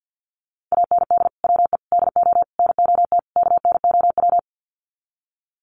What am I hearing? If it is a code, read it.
Morse code, heard as a keyed tone, about 51 wpm, 700 Hz. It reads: AND WE DO NOT KNOW
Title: Sets of 5 Words Encore 50wpm